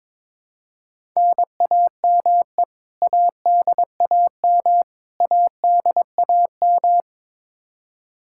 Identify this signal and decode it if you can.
Morse code, heard as a keyed tone, about 22 wpm, 700 Hz. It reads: NAME ADAM ADAM
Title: K1USN SST QSO Practice 22wpm